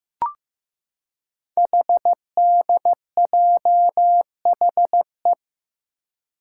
Morse code, heard as a keyed tone, about 15 wpm, 700 Hz. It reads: HDJHE
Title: Code Groups - 5 Letters - 15wpm